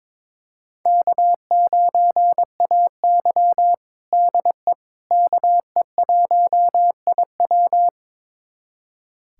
Morse code, heard as a keyed tone, about 21 wpm, 700 Hz. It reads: K9AY DE KE1IW
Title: USA Callsign DE Callsign 22wpm